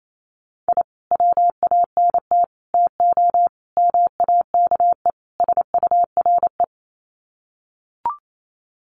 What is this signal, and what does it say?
Morse code, heard as a keyed tone, about 28 wpm, 700 Hz. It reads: I WANT TO MAKE SURE
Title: Sets of 5 Words Encore2 28wpm